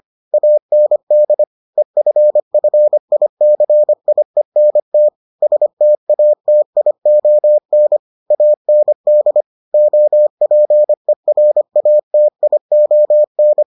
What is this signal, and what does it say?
Morse code, heard as a keyed tone, about 25 wpm, 600 Hz. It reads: AND EFFICIENT STATION AND OPERATION